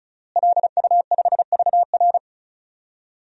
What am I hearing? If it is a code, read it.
Morse code, heard as a keyed tone, about 36 wpm, 700 Hz. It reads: LU5VR
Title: DXCC-100 Call Signs - 5 Characters - Repeated 3x - 35wpm